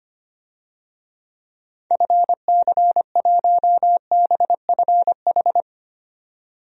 Morse code, heard as a keyed tone, about 25 wpm, 700 Hz. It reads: FC1BFH
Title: DXCC-100 Call Signs - 6 Characters - 25wpm